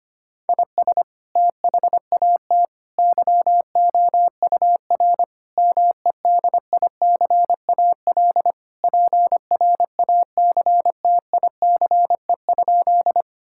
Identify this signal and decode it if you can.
Morse code, heard as a keyed tone, about 25 wpm, 700 Hz. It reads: IS THAT YOUR MEDICAL PRACTICE?
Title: Sentences from Top 1000 Words - Review Part 5 - 25wpm